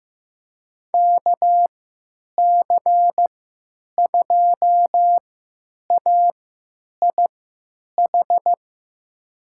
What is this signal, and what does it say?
Morse code, heard as a keyed tone, about 15 wpm, spaced as for 9 wpm, 700 Hz. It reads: KC2AIH
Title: USA Call Signs - 6 Characters - 3x Character Spacing - 15wpm